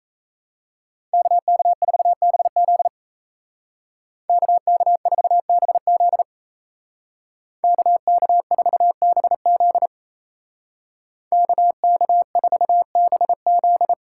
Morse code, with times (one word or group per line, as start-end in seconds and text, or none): 1.1-2.9 KK4BZ
4.3-6.2 KK4BZ
7.6-9.9 KK4BZ
11.3-13.9 KK4BZ